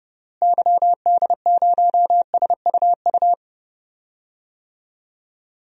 Morse code, heard as a keyed tone, about 30 wpm, 700 Hz. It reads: YD0SUU
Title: DXCC-100 Call Signs - 6 Characters - Repeated 2x - 30wpm